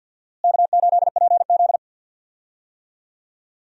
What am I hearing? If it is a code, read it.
Morse code, heard as a keyed tone, about 50 wpm, 700 Hz. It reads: K8PZ
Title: USA Call Signs - 4 Characters - Repeated 3x - 50wpm